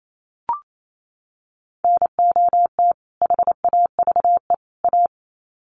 Morse code, heard as a keyed tone, about 28 wpm, 700 Hz. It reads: NOT HAVE A
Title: Sets of 3 Words 28wpm